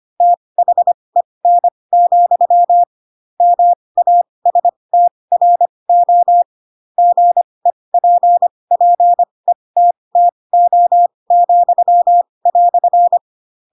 Morse code, heard as a keyed tone, about 25 wpm, spaced as for 20 wpm, 700 Hz. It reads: THEN, MASTRO GEPPETTO,"
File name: Pinocchio / Pinocchio ch2 15 25 ik1hge